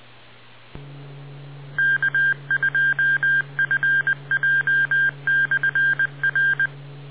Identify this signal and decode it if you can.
Morse code, heard as a keyed tone, about 20 wpm, 1600 Hz. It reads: K2FJ/R